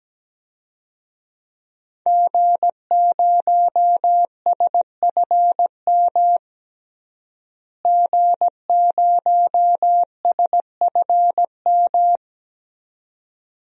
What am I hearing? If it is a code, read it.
Morse code, heard as a keyed tone, about 17 wpm, 700 Hz. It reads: G0SFM G0SFM
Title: DXCC-100 Call Signs - 5 Characters - Repeated 2x - 17wpm